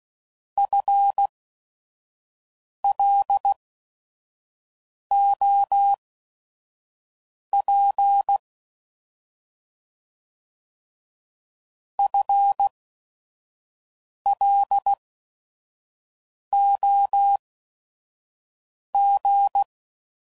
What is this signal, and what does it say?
Morse code, heard as a keyed tone, about 16 wpm, spaced as for 5 wpm, 800 Hz. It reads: FLOP FLOG
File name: Prefixes / By First Letter / Fs